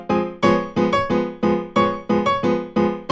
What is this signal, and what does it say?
There is no code.